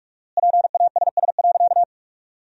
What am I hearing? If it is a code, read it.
Morse code, heard as a keyed tone, about 46 wpm, 700 Hz. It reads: PASS.
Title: 4x Word-Spacing - Sentences from Top 400 Words - 45wpm